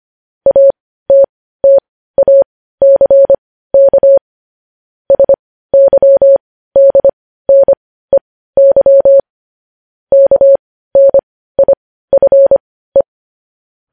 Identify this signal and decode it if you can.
Morse code, heard as a keyed tone, about 25 wpm, spaced as for 15 wpm, 550 Hz. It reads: ATTACK SYDNEY KNIFE